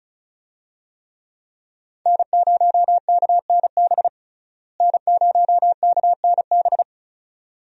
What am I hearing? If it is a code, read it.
Morse code, heard as a keyed tone, about 36 wpm, 700 Hz. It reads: N0KNB N0KNB
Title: USA Call Signs - 5 Characters - Repeated 2x - 35wpm